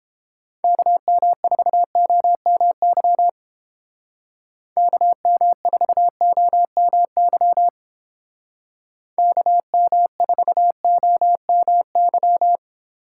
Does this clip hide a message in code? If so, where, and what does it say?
Yes: Morse code from 0.6 to 12.6 s, heard as a keyed tone, about 31 wpm, 700 Hz. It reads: KM4OMY KM4OMY KM4OMY